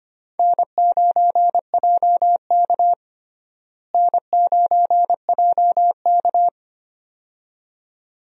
Morse code, heard as a keyed tone, about 25 wpm, 700 Hz. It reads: N9JK N9JK
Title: USA Call Signs - 4 Characters - Repeated 2x - 25wpm